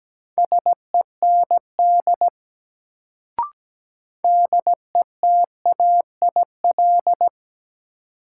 Morse code, heard as a keyed tone, about 17 wpm, 700 Hz. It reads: SEND DETAIL
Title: Sentences from Top 1500 Words - New Word Drill - 17wpm